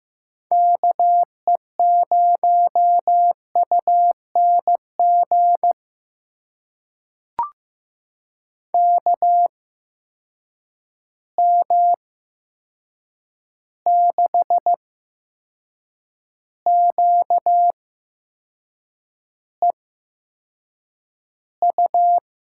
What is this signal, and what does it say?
Morse code, heard as a keyed tone, about 15 wpm, spaced as for 6 wpm, 700 Hz. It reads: KE0UNG KM6QEU
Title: USA Call Signs - 6 Characters - 8x Character Spacing - 15wpm